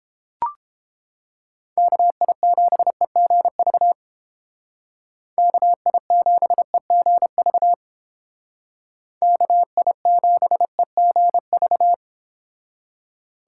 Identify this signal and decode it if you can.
Morse code, heard as a keyed tone, about 28 wpm, 700 Hz. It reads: KI7EGV KI7EGV KI7EGV